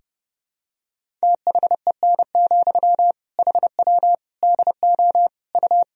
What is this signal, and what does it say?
Morse code, heard as a keyed tone, about 32 wpm, 700 Hz. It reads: THEN, HW DO U
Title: Sentences from Top 100 Words and Abbreviations 30wpm